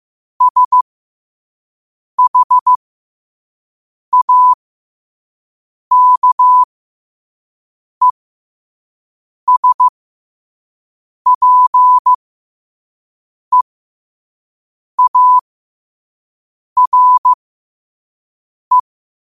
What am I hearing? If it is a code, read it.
Morse code, heard as a keyed tone, about 15 wpm, spaced as for 5 wpm, 1000 Hz. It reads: SHAKESPEARE
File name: qotdfiles / 5wpm